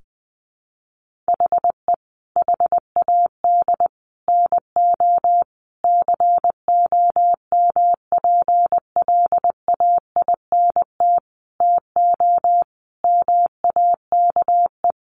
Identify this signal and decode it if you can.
Morse code, heard as a keyed tone, about 20 wpm, 700 Hz. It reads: HE HAD NO COMPLAINT TO MAKE
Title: Sentences from Top 1400 Words 20wpm